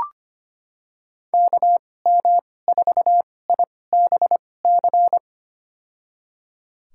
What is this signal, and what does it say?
Morse code, heard as a keyed tone, about 25 wpm, spaced as for 18 wpm, 700 Hz. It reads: KM4IBC
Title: USA Call Signs - 2x Character Spacing - 25wpm